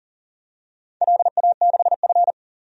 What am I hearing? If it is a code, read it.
Morse code, heard as a keyed tone, about 40 wpm, 700 Hz. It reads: LA6F
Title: DXCC-100 Call Signs - 4 Characters - Repeated 3x - 40wpm